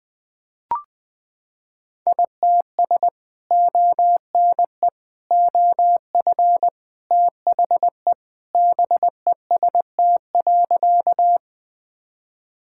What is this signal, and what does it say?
Morse code, heard as a keyed tone, about 20 wpm, 700 Hz. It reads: ITS ONE OF THE BEST.
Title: Sentences with 3-Word Phrases - 20wpm